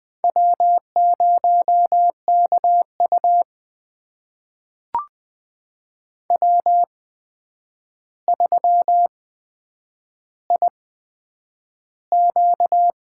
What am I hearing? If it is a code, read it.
Morse code, heard as a keyed tone, about 20 wpm, spaced as for 9 wpm, 700 Hz. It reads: W0KU W3IQ
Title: USA Call Signs - 4 Characters - 8x Character Spacing - 20wpm